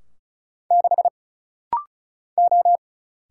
Morse code, heard as a keyed tone, about 35 wpm, 700 Hz. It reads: 6 O
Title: Single Letter-Number Alt - Rapid-Fire 35wpm